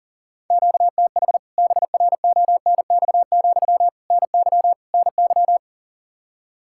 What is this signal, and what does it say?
Morse code, heard as a keyed tone, about 40 wpm, 700 Hz. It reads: QTH BRONX, NY NY
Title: K1USN SST QSO Practice 40wpm